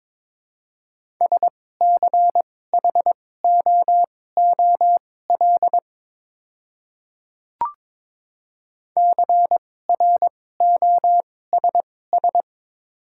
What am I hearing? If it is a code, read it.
Morse code, heard as a keyed tone, about 22 wpm, spaced as for 16 wpm, 700 Hz. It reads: SCHOOL CROSS
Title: 2x Character Spacing - Top 300 Words - 22wpm